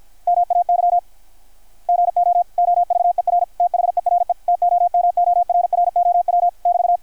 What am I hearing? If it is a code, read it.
Morse code, heard as a keyed tone, about 50 wpm, 700 Hz. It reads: MAY GO OVER THERE TOMORROW <BT>
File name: archived / CRSnet